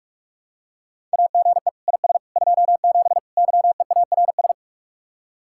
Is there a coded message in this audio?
Yes: Morse code from 1.1 to 4.5 s, heard as a keyed tone, about 48 wpm, 700 Hz. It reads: AGE IS 27 YEARS